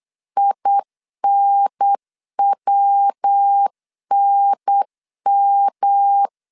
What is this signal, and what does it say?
Morse code, heard as a keyed tone, about 8 wpm, 800 Hz. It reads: INWNM
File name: CW / DET / 1 3 1500C S B 8wpm 4141